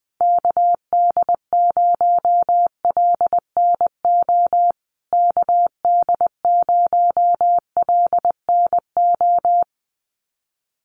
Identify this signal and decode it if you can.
Morse code, heard as a keyed tone, about 20 wpm, 700 Hz. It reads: KD0LNO KD0LNO
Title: USA Call Signs Repeated 3x 20wpm